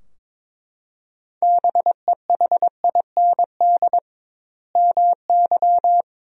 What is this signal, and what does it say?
Morse code, heard as a keyed tone, about 21 wpm, 700 Hz. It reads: BEHIND MY
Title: 2x Word-Spacing - Sentences from Top 500 Words - 22wpm